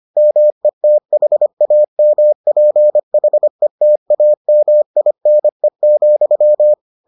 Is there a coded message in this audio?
Yes: Morse code from 0.2 to 6.7 s, heard as a keyed tone, about 25 wpm, 600 Hz. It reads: METHAMPHETAMINE,